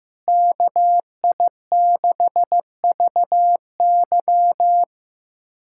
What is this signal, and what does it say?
Morse code, heard as a keyed tone, about 16 wpm, 700 Hz. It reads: KI6VY
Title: USA Callsign DE Callsign 15wpm